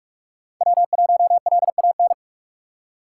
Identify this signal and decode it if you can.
Morse code, heard as a keyed tone, about 46 wpm, 700 Hz. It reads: W1LAN
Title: USA Call Signs - 5 Characters - 4x Character Spacing - 45wpm